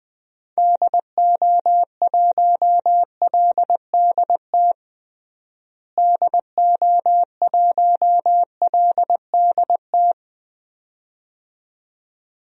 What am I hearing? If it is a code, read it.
Morse code, heard as a keyed tone, about 20 wpm, 700 Hz. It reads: DO1LDT DO1LDT
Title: DXCC-100 Call Signs - 6 Characters - Repeated 2x - 20wpm